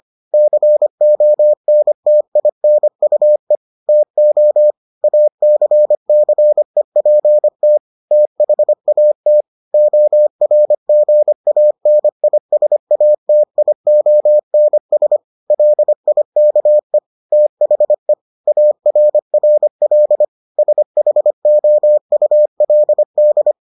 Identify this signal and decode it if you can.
Morse code, heard as a keyed tone, about 25 wpm, 600 Hz. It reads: CONTINUE TO ACCEPT THAT ORGANISATIONS LIKE THE ARRL SHOULD